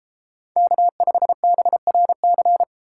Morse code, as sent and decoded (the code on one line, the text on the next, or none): -.- ..... -... .-. -.-.
K5BRC